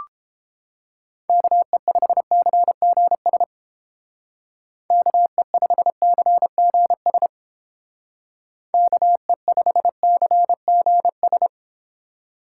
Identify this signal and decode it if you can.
Morse code, heard as a keyed tone, about 31 wpm, 700 Hz. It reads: KE5CGS KE5CGS KE5CGS